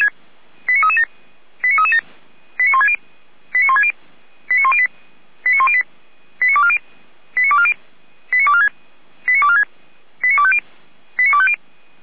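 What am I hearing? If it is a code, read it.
Five-tone selective call (ZVEI), heard as quick runs of five tones, tones of about 70 ms. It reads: 89337 89337 89160 89160 89108 89108 89240 89240 89246 89246 89250 89250
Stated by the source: Received by SDR; ZVEI five-tone code 89075.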